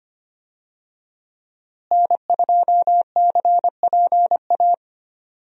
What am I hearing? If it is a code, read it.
Morse code, heard as a keyed tone, about 25 wpm, 700 Hz. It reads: N2CPA
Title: USA Call Signs - 5 Characters - Repeated 3x - 25wpm